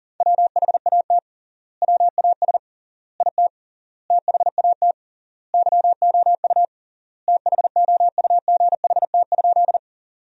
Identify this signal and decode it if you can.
Morse code, heard as a keyed tone, about 37 wpm, 700 Hz. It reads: WHAT WAS IT THAT YOU THOUGHT?
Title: 3x Word-Spacing - Sentences from Top 300 Words - 40wpm